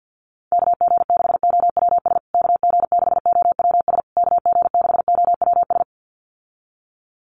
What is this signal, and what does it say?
Morse code, heard as a keyed tone, about 50 wpm, 700 Hz. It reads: KG6OWS KG6OWS KG6OWS
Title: USA Call Signs Repeated 3x 50wpm